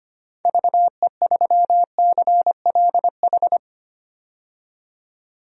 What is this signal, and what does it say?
Morse code, heard as a keyed tone, about 25 wpm, 690 Hz. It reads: VE3CLH
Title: DXCC-100 Call Signs - 6 Characters - 25wpm